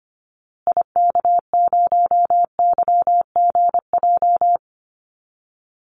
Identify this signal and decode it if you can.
Morse code, heard as a keyed tone, about 25 wpm, 700 Hz. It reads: IK0YGJ
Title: Call Signs Repeated 2x - 25wpm